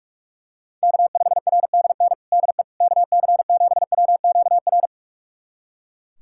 Morse code, heard as a keyed tone, about 42 wpm, 700 Hz. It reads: K5RDN DE KC7WQR
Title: USA Callsign DE Callsign 45wpm